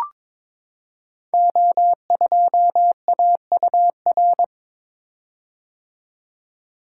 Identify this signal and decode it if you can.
Morse code, heard as a keyed tone, about 22 wpm, 700 Hz. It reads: O2AUR
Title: Code Groups - 5 Letter:Numbers - 22wpm